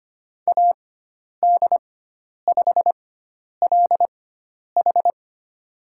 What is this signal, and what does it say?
Morse code, heard as a keyed tone, about 25 wpm, spaced as for 10 wpm, 700 Hz. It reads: AD5LH
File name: Lesson 16g - Letter c - Call Signs 25wpm